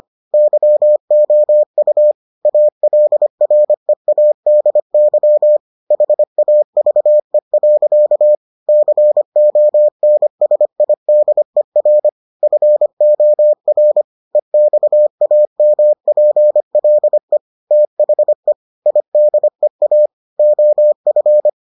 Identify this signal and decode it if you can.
Morse code, heard as a keyed tone, about 25 wpm, 600 Hz. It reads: YOU ALREADY HAVE. CONSIDER FOR EXAMPLE THE IDEA OF